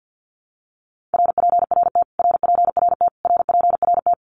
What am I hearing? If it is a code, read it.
Morse code, heard as a keyed tone, about 50 wpm, 700 Hz. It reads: RPRT RPRT RPRT
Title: Abbreviations Repeated 3x - 50wpm